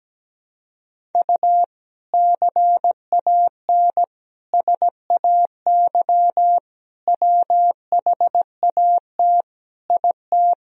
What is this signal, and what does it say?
Morse code, heard as a keyed tone, about 17 wpm, 700 Hz. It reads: U CAN SAY WHAT IT
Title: Sentences from Top 200 Words and Abbreviations 17wpm